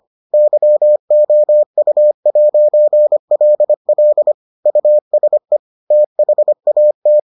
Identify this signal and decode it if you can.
Morse code, heard as a keyed tone, about 25 wpm, 600 Hz. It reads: YOU'LL USE THAT